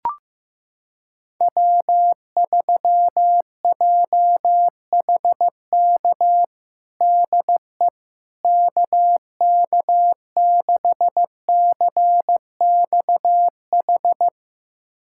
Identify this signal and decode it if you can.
Morse code, heard as a keyed tone, about 15 wpm, 700 Hz. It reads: W3JHK DE KK6CXH
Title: USA Callsign DE Callsign 15wpm